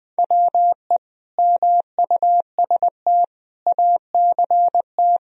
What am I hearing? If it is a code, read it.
Morse code, heard as a keyed tone, about 20 wpm, 700 Hz. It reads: WE MUST ACT
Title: Sentences from Top 1000 Words - Review Part 1 - 20wpm